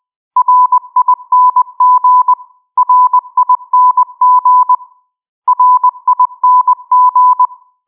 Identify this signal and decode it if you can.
Morse code, heard as a keyed tone, about 20 wpm, 1000 Hz. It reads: RING RING RING